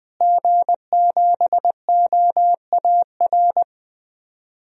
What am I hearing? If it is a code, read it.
Morse code, heard as a keyed tone, about 20 wpm, 700 Hz. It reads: G7OAR
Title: DXCC-100 Call Signs - 5 Characters - Repeated 3x - 20wpm